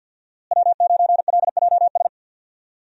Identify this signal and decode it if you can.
Morse code, heard as a keyed tone, about 52 wpm, 700 Hz. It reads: W9LJS
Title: USA Call Signs - 5 Characters - Repeated 3x - 50wpm